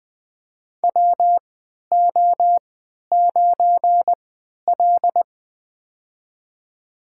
Morse code, heard as a keyed tone, about 20 wpm, spaced as for 11 wpm, 700 Hz. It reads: WO9L
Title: USA Call Signs - 4 Characters - 3x Character Spacing - 20wpm